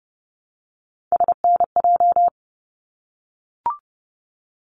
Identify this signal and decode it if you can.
Morse code, heard as a keyed tone, about 30 wpm, 700 Hz. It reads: SNJ
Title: ARRL Sections 30wpm